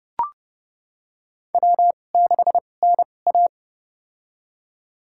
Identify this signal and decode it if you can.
Morse code, heard as a keyed tone, about 29 wpm, spaced as for 22 wpm, 700 Hz. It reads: W6NA
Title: USA Call Signs - 4 Characters - 2x Character Spacing - 30wpm